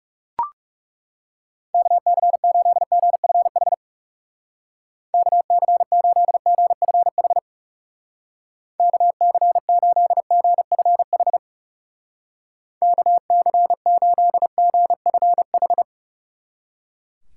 Morse code, heard as a keyed tone, about 37 wpm, 700 Hz. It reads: KC8GFH KC8GFH KC8GFH KC8GFH
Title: Speed Racing - USA 2x3 Call Signs 30wpm